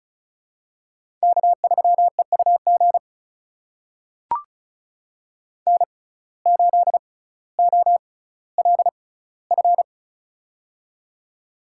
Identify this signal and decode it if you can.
Morse code, heard as a keyed tone, about 36 wpm, spaced as for 18 wpm, 700 Hz. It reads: K3EUG N8OLF
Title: USA Call Signs - 5 Characters - 6x Character Spacing - 35wpm